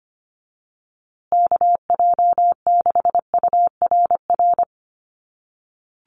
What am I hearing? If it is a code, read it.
Morse code, heard as a keyed tone, about 25 wpm, 700 Hz. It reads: KJ6URR
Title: USA Call Signs Repeated 2x 25wpm